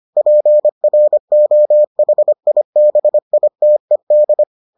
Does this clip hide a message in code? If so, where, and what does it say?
Yes: Morse code from 0.2 to 4.4 s, heard as a keyed tone, about 25 wpm, 600 Hz. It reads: PROHIBITED